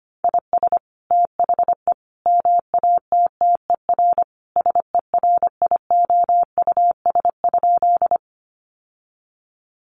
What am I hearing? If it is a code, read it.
Morse code, heard as a keyed tone, about 25 wpm, 700 Hz. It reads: IS THE MATTER SERIOUS?
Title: Sentences from Top 1100 Words 25wpm